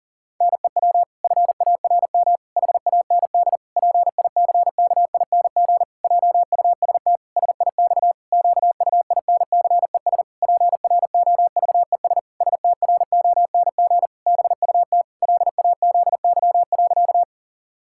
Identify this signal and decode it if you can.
Morse code, heard as a keyed tone, about 37 wpm, 700 Hz. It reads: NEW FARM HAND PICKING JUST SIX QUINCES PROVES STRONG BUT LAZY.
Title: Pangram Challenge v2 - 40wpm